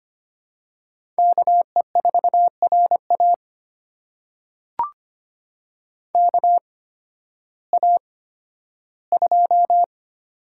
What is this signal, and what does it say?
Morse code, heard as a keyed tone, about 25 wpm, spaced as for 13 wpm, 700 Hz. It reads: KE4RA KA2
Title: USA Call Signs - 8x Character Spacing - 25wpm